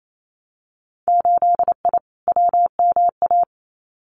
Morse code, heard as a keyed tone, about 28 wpm, 700 Hz. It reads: 8I WMA
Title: Winter Field Day Exchanges Encore 28wpm